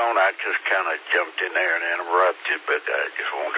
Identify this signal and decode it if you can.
Emergency Alert System (SAME) tones, heard as harsh buzzing bursts, with speech. It is heard from 0.3 to 3.2 s.